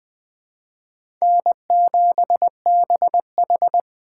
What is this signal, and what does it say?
Morse code, heard as a keyed tone, about 21 wpm, 700 Hz. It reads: N7BH